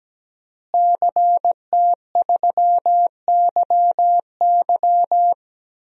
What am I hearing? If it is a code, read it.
Morse code, heard as a keyed tone, about 17 wpm, 700 Hz. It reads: CT3YY
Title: DXCC-100 Call Signs - 5 Characters - Repeated 3x - 17wpm